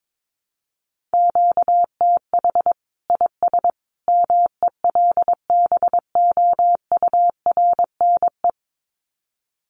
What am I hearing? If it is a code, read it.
Morse code, heard as a keyed tone, about 22 wpm, 700 Hz. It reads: QTH IS MELBOURNE
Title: Common QSO Phrases 22wpm